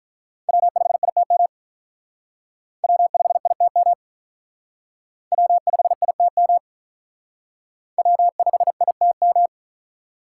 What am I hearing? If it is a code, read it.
Morse code, heard as a keyed tone, about 42 wpm, 700 Hz. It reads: W5ITM W5ITM W5ITM W5ITM
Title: Speed Racing - USA 1x3 Call Signs 35wpm